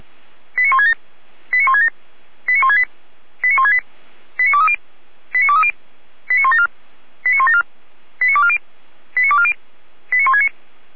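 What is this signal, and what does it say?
Five-tone selective call (ZVEI), heard as quick runs of five tones, tones of about 70 ms. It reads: 89167 89167 89168 89168 89230 89230 89174 89174 89240 89240 89169
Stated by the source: ZVEI five-tone call, code 89130, received by SDR.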